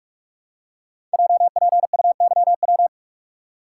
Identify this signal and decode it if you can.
Morse code, heard as a keyed tone, about 46 wpm, 700 Hz. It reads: JPUYW
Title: Code Groups - 5 Letters - 45wpm